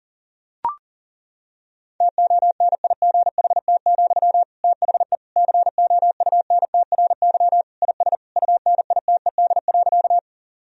Morse code, heard as a keyed tone, about 37 wpm, 700 Hz. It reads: TONIGHT, THE COUNTRY IS UNITED.
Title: Sentences from Top 1000 Words - Review Part 4 - 40wpm